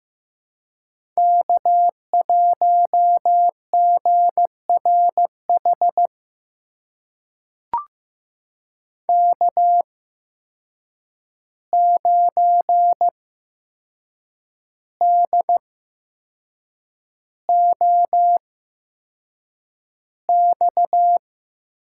Morse code, heard as a keyed tone, about 15 wpm, spaced as for 6 wpm, 700 Hz. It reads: K1GRH K9DOX
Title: USA Call Signs - 5 Characters - 8x Character Spacing - 15wpm